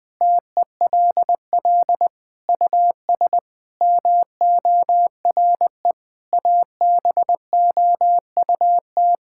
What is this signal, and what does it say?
Morse code, heard as a keyed tone, about 21 wpm, 700 Hz. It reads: TELL US MORE ABOUT
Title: Sentences from Top 1600 Words 20wpm